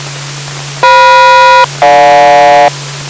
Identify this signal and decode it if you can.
Aircraft SELCAL, heard as two pairs of tones. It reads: FM-GJ